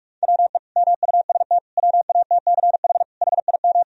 Morse code, heard as a keyed tone, about 46 wpm, 700 Hz. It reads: WE MUST WATCH HIM
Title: Sentences from Top 1000 Words - Review Part 3 - 45wpm